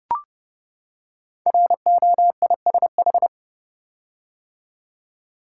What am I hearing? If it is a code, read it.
Morse code, heard as a keyed tone, about 29 wpm, 700 Hz. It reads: ROISH
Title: Code Groups - 5 Letters - 30wpm